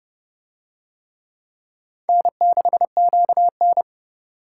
Morse code, heard as a keyed tone, about 29 wpm, 700 Hz. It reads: N6QN